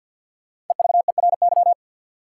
Morse code, heard as a keyed tone, about 53 wpm, 700 Hz. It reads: EVERY
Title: Sentences from Top 200 Words - New Word Drill - 50wpm